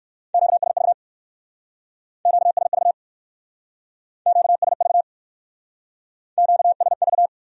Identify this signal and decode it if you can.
Morse code, heard as a keyed tone, about 55 wpm, 700 Hz. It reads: QSV QSV QSV QSV
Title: Speed-Racing Q-Codes - 45wpm